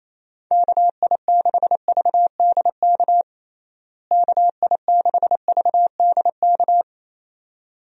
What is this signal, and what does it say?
Morse code, heard as a keyed tone, about 27 wpm, 700 Hz. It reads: KI6VDK KI6VDK
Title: USA Call Signs - 6 Characters - Repeated 2x - 28wpm